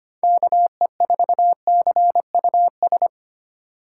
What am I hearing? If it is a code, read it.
Morse code, heard as a keyed tone, about 25 wpm, 700 Hz. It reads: KE4CUS
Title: USA Call Signs - 2x Character Spacing - 25wpm